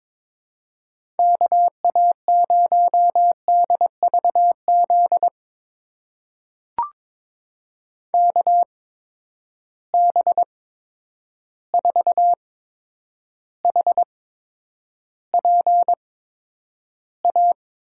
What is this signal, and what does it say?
Morse code, heard as a keyed tone, about 22 wpm, spaced as for 9 wpm, 700 Hz. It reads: KA0DVZ KB4HPA
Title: USA Call Signs - 6 Characters - 8x Character Spacing - 22wpm